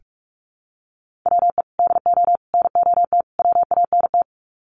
Morse code, heard as a keyed tone, about 44 wpm, 700 Hz. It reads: WE DO NOT WANT
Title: Sets of 4 Words 45wpm